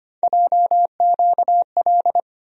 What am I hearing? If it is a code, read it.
Morse code, heard as a keyed tone, about 25 wpm, 700 Hz. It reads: JQL
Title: USA Call Sign Postfixes - 3 Letters 25wpm